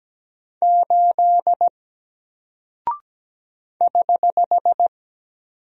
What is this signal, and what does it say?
Morse code, heard as a keyed tone, about 17 wpm, 700 Hz. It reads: 8 <HH>
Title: Single Character (Letter:Number:Punctuation:Prosign) 17wpm